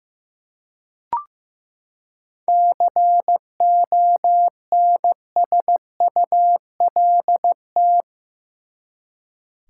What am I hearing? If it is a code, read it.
Morse code, heard as a keyed tone, about 15 wpm, 700 Hz. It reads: CONSULT
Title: Seven Letter Words - 15wpm